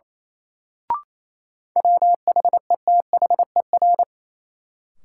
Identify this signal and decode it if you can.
Morse code, heard as a keyed tone, about 28 wpm, 700 Hz. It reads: WHETHER
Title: Sentences from Top 1000 Words - Review Part 3 - New Word Drill - 28wpm